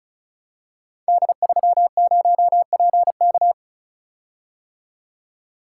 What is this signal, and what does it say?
Morse code, heard as a keyed tone, about 36 wpm, 690 Hz. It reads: D30PK
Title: Code Groups - 5 Letter:Numbers - 35wpm